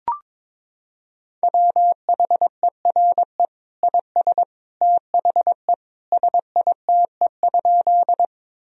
Morse code, heard as a keyed tone, about 22 wpm, 700 Hz. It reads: WHERE IS THE SITE?
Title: Sentences from Top 1000 Words - Review Part 4 - 22wpm